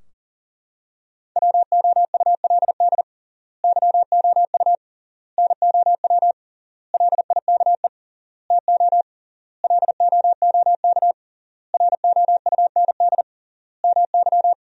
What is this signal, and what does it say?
Morse code, heard as a keyed tone, about 40 wpm, 700 Hz. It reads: WOULD YOU NOW LIKE TO LOOK ROUND MY
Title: 3x Word-Spacing - Sentences from Top 200 Words - 40wpm